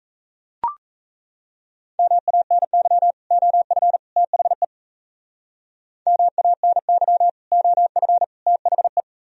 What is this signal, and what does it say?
Morse code, heard as a keyed tone, about 34 wpm, 700 Hz. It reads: MANY OF THE MANY OF THE
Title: Speed-Racing - Sets of 3 Words 28wpm